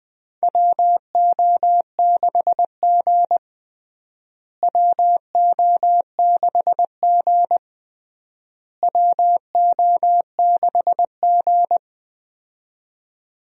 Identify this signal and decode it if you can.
Morse code, heard as a keyed tone, about 20 wpm, 700 Hz. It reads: WO6G WO6G WO6G
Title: USA Call Signs - 4 Characters - Repeated 3x - 20wpm